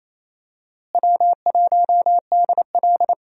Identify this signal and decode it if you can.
Morse code, heard as a keyed tone, about 28 wpm, 700 Hz. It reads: W1DL
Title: USA Call Signs - 4 Characters - Repeated 2x - 28wpm